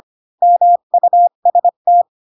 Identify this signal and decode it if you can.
Morse code, heard as a keyed tone, about 25 wpm, 700 Hz. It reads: MUST
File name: Pinocchio / Pinocchio ch5 20 25 ik1hge